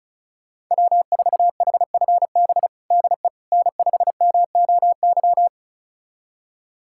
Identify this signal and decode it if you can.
Morse code, heard as a keyed tone, about 36 wpm, 700 Hz. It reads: W4HFB DE N5MOY
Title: USA Callsign DE Callsign - 5 Characters - 35wpm